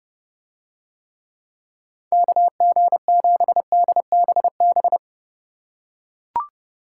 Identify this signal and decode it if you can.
Morse code, heard as a keyed tone, about 31 wpm, 700 Hz. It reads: KG7DBB